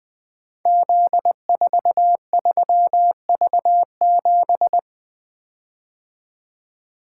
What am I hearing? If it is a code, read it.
Morse code, heard as a keyed tone, about 20 wpm, 700 Hz. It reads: Z43V7
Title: Code Groups - 5 Letter:Numbers - 20wpm